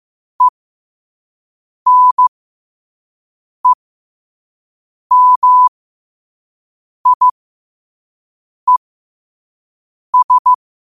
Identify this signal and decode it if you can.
Morse code, heard as a keyed tone, about 15 wpm, spaced as for 5 wpm, 1000 Hz. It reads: ENEMIES
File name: qotdfiles / 5wpm